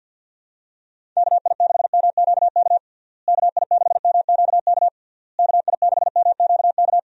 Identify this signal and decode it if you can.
Morse code, heard as a keyed tone, about 50 wpm, 700 Hz. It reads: KI6MQK KI6MQK KI6MQK
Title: USA Call Signs - 6 Characters - Repeated 3x - 50wpm